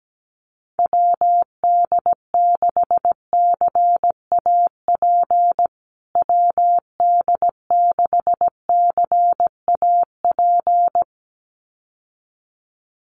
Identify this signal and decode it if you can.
Morse code, heard as a keyed tone, about 17 wpm, 700 Hz. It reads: WD6CAP WD6CAP
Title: USA Call Signs Repeated 2x 17wpm